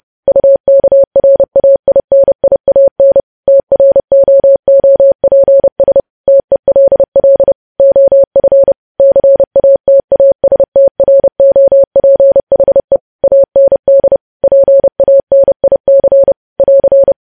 Morse code, heard as a keyed tone, about 30 wpm, 550 Hz. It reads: UKRAINIAN TROOPS TELL OF CATASTROPHE AND PANIC <AR>
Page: news headlines in morse code at 30 wpm